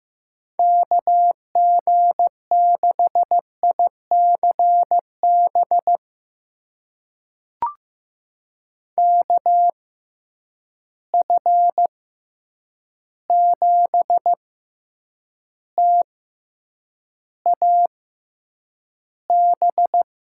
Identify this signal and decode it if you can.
Morse code, heard as a keyed tone, about 15 wpm, spaced as for 8 wpm, 700 Hz. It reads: KG6ICB KF7TAB
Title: USA Call Signs - 6 Characters - 6x Character Spacing - 15wpm